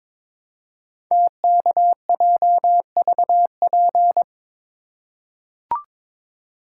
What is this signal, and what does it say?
Morse code, heard as a keyed tone, about 22 wpm, 700 Hz. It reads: TKJVP
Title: Code Groups - 5 Letters - 22wpm